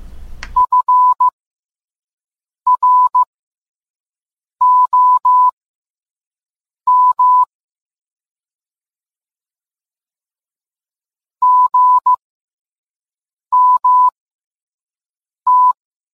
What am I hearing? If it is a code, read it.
Morse code, heard as a keyed tone, about 15 wpm, spaced as for 5 wpm, 1000 Hz. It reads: FROM GMT
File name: posts / gmt-games-50-off-coupon-if-you-can-read-it